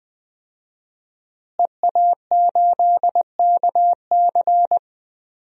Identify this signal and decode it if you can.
Morse code, heard as a keyed tone, about 20 wpm, 700 Hz. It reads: EA8KC
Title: DXCC-100 Call Signs - 5 Characters - Repeated 2x - 20wpm